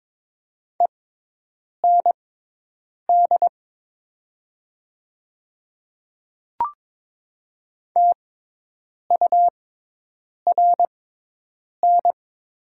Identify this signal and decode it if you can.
Morse code, heard as a keyed tone, about 22 wpm, spaced as for 8 wpm, 700 Hz. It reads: END TURN
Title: 6x Character Spacing - Top 300 Words - 22wpm